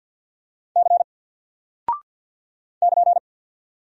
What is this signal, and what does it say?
Morse code, heard as a keyed tone, about 48 wpm, 700 Hz. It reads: C <KN>
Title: Single Character (Letter:Number:Punctuation:Prosign) Alt 50wpm